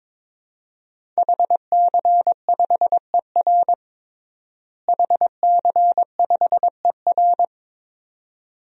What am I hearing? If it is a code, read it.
Morse code, heard as a keyed tone, about 22 wpm, 700 Hz. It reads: HC5ER HC5ER
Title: DXCC-100 Call Signs - 5 Characters - Repeated 2x - 22wpm